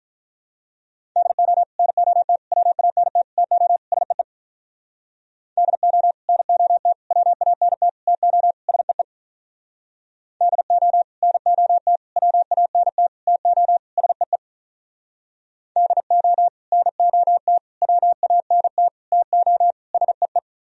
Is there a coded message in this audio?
Yes: Morse code from 1.2 to 20.4 s, heard as a keyed tone, about 45 wpm, 700 Hz. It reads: DO NOT WANT TO SEE DO NOT WANT TO SEE DO NOT WANT TO SEE DO NOT WANT TO SEE